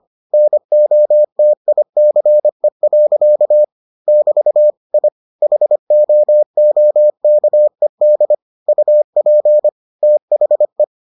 Morse code, heard as a keyed tone, about 25 wpm, 600 Hz. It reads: NOTICE. <BT> I HOOKED UP THE